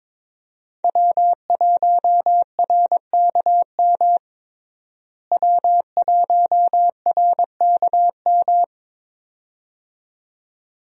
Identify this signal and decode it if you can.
Morse code, heard as a keyed tone, about 21 wpm, 700 Hz. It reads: W1RKM W1RKM
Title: USA Call Signs - 5 Characters - Repeated 2x - 22wpm